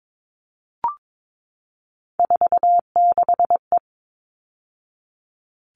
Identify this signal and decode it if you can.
Morse code, heard as a keyed tone, about 22 wpm, 700 Hz. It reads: 46E
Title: All Cut Numbers - 3-Digit 22wpm